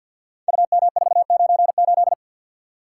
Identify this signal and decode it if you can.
Morse code, heard as a keyed tone, about 47 wpm, 700 Hz. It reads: UM498
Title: Code Groups - 5 Letter:Numbers - 50wpm